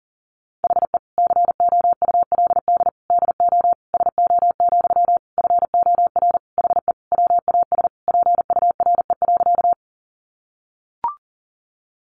Morse code, heard as a keyed tone, about 40 wpm, 700 Hz. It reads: HE COULD DO SO, FOR HE WAS PURE.
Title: Sentences from Top 1500 Words 40wpm